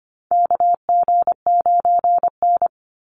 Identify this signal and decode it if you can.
Morse code, heard as a keyed tone, about 25 wpm, 700 Hz. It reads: KG9N
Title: Call Signs Repeated 3x - 25wpm